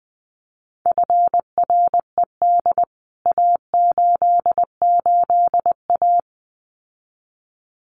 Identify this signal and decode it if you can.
Morse code, heard as a keyed tone, about 20 wpm, 700 Hz. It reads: FRED A88A
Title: CWT Exchange 20wpm